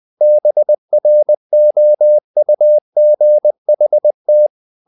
Morse code, heard as a keyed tone, about 20 wpm, 600 Hz. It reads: BROUGHT
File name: the old man and the sea part 2 20wpm